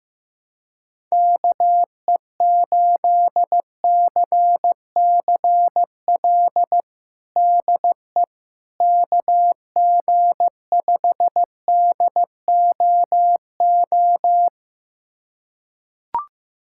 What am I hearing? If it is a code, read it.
Morse code, heard as a keyed tone, about 15 wpm, 700 Hz. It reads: KE8CCL DE KG5DOO
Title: USA Callsign DE Callsign - 6 Characters - 15wpm